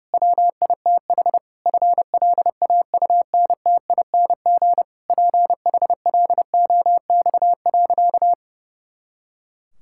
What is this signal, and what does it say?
Morse code, heard as a keyed tone, about 30 wpm, 700 Hz. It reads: WITH FLAUNTING PHLOX.
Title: Pangram Challenge v2 - 30wpm